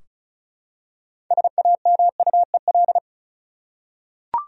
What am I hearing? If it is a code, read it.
Morse code, heard as a keyed tone, about 34 wpm, 700 Hz. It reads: SAMUEL
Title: USA Common Names - 35wpm